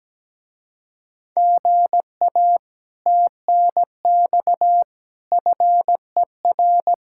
Morse code, heard as a keyed tone, about 17 wpm, 700 Hz. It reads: GA TNX FER
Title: Speed-Racing Common QSO Phrases - 17wpm